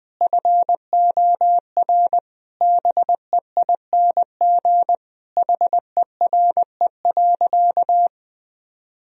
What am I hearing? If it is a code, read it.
Morse code, heard as a keyed tone, about 20 wpm, 700 Hz. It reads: FOR BEING HERE.
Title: Sentences from Top 1000 Words - Review Part 3 - 20wpm